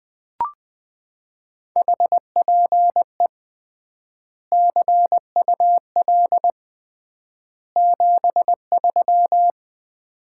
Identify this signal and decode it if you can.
Morse code, heard as a keyed tone, about 20 wpm, 700 Hz. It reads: HPE CUL 73
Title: Common QSO Phrases - 3x Word-Spacing - 20wpm